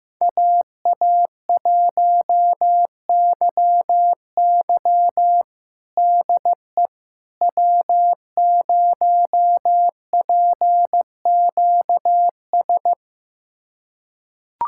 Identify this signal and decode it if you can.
Morse code, heard as a keyed tone, about 15 wpm, 700 Hz. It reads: AA1YY DE W0PQS
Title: USA Callsign DE Callsign - 5 Characters - 15wpm